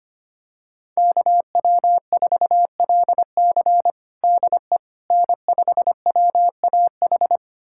Morse code, heard as a keyed tone, about 25 wpm, 700 Hz. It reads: KW4LC DE N5WAH
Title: USA Callsign DE Callsign - 5 Characters - 25wpm